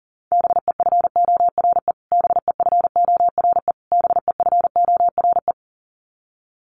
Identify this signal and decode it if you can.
Morse code, heard as a keyed tone, about 40 wpm, 700 Hz. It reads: BEFORE BEFORE BEFORE
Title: Top 200 Words Repeated 40wpm